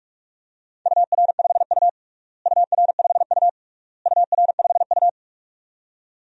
Morse code, heard as a keyed tone, about 41 wpm, 700 Hz. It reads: UR5U UR5U UR5U
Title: DXCC-100 Call Signs - 4 Characters - Repeated 3x - 45wpm